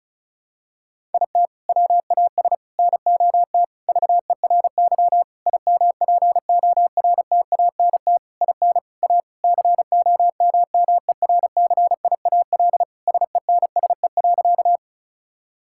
Morse code, heard as a keyed tone, about 36 wpm, 700 Hz. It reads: IT WAS NOT VERY IMPORTANT IN A COMMERCIAL SENSE.
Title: Sentences from Top 1000 Words - Review Part 5 - 35wpm